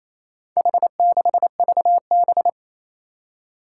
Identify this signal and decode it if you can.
Morse code, heard as a keyed tone, about 28 wpm, 700 Hz. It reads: H6VB
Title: Four Challenging Letter-Number Combinations 28wpm